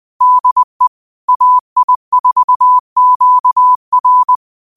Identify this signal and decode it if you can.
Morse code, heard as a keyed tone, about 20 wpm, 1000 Hz. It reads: DE AI4QR